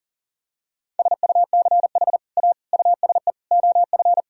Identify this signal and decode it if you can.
Morse code, heard as a keyed tone, about 40 wpm, 700 Hz. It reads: SUCH A USE OF
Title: Sentences from Top 1000 Words - Review Part 3 - 40wpm